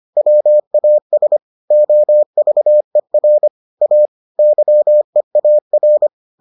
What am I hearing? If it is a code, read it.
Morse code, heard as a keyed tone, about 25 wpm, 600 Hz. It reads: WAS OVER A YEAR